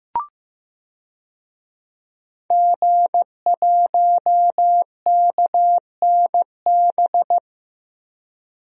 Morse code, heard as a keyed tone, about 15 wpm, 700 Hz. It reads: G1KNB
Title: DXCC-100 Call Signs - 5 Characters - Repeated 2x - 15wpm